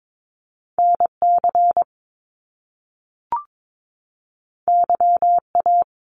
Morse code, heard as a keyed tone, about 22 wpm, 700 Hz. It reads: NC YA
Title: Top Two Letter Combinations in Words 22wpm